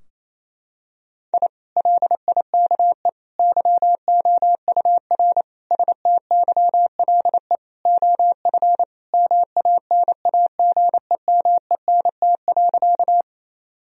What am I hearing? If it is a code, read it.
Morse code, heard as a keyed tone, about 28 wpm, 700 Hz. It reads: I LIKE YOUR STYLE OF MANAGEMENT.
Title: Sentences from Top 1000 Words - Review Part 4 - 28wpm